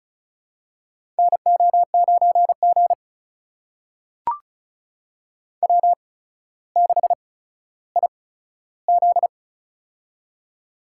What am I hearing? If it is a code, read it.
Morse code, heard as a keyed tone, about 34 wpm, spaced as for 15 wpm, 700 Hz. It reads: NO9G W6IZ
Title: USA Call Signs - 8x Character Spacing - 35wpm